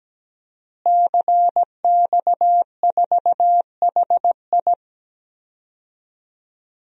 Morse code, heard as a keyed tone, about 17 wpm, 700 Hz. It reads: CX4HI
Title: DXCC-100 Call Signs - 5 Characters - Repeated 2x - 17wpm